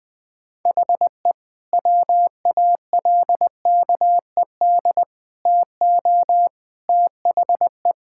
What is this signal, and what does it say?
Morse code, heard as a keyed tone, about 21 wpm, 700 Hz. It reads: HE WALKED TO THE
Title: Sentences with 5-Word Phrases - 20wpm